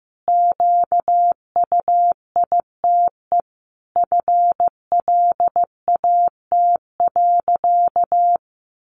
Morse code, heard as a keyed tone, about 15 wpm, 700 Hz. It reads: QUITE FLAT.
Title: Sentences from Top 1000 Words 15wpm